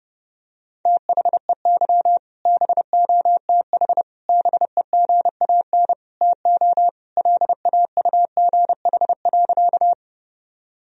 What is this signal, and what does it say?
Morse code, heard as a keyed tone, about 32 wpm, 700 Hz. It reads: THEY BOTH BEGAN TO LAUGH.
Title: Sentences from Top 1000 Words - Review Part 1 - 30wpm